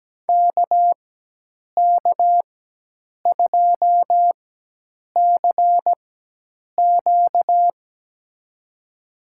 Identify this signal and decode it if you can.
Morse code, heard as a keyed tone, about 17 wpm, spaced as for 8 wpm, 700 Hz. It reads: KK2CQ
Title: USA Call Signs - 5 Characters - 4x Character Spacing - 17wpm